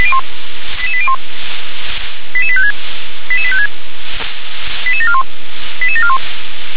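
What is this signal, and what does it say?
Five-tone selective call (ZVEI), heard as quick runs of five tones, tones of about 70 ms. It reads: 88801 88856 88856 88741 88741
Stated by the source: ZVEI five-tone call 88824 received on SDR.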